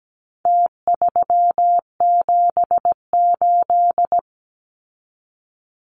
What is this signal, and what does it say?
Morse code, heard as a keyed tone, about 17 wpm, 700 Hz. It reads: T378
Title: Numbers - Contest Style - 4-Digit 17wpm